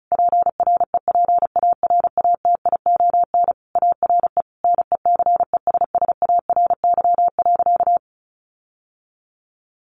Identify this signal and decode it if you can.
Morse code, heard as a keyed tone, about 34 wpm, 700 Hz. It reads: PREPARATION ARE NECESSARY.
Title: Sentences from Top 1400 Words 35wpm